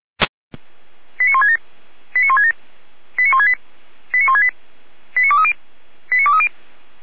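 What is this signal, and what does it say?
Five-tone selective call (ZVEI), heard as quick runs of five tones, tones of about 70 ms. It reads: 89167 89167 89168 89168 89230 89230
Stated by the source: ZVEI five-tone selective call 89121 received on an SDR.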